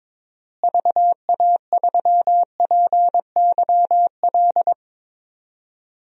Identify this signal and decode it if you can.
Morse code, heard as a keyed tone, about 21 wpm, 700 Hz. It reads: VA3PYL
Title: DXCC-100 Call Signs - 6 Characters - Repeated 3x - 22wpm